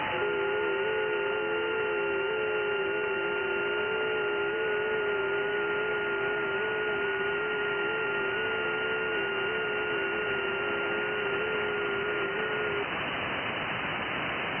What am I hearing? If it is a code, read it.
FT8.
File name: IC-7100